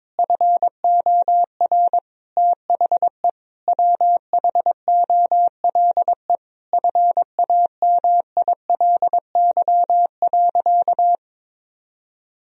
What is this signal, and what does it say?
Morse code, heard as a keyed tone, about 22 wpm, 700 Hz. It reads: FOR THE WHOLE FAMILY.
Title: Sentences from Top 1000 Words - Review Part 4 - 22wpm